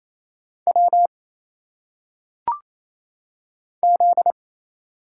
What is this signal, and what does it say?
Morse code, heard as a keyed tone, about 27 wpm, 700 Hz. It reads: W Z
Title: Single Letters - Phonetic Pronunciation 28wpm